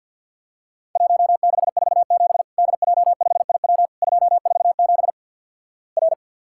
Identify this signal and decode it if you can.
Morse code, heard as a keyed tone, about 54 wpm, 700 Hz. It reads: 1647 DJ5IW 247 R
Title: Worked All Europe DX Contest - QTC Exchange with Repeats - 50wpm